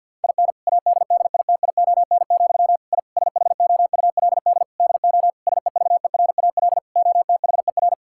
Morse code, heard as a keyed tone, about 52 wpm, 700 Hz. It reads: IN ADDITION, I SHOULD DO SEVERAL OTHER
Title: Sentences from Top 1000 Words - Review Part 6 - 50wpm